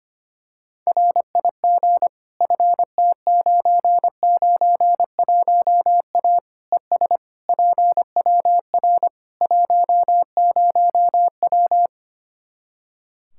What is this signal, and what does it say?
Morse code, heard as a keyed tone, about 25 wpm, 700 Hz. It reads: RIG FT991A ES PWR 10W
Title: K1USN SST QSO Practice 25wpm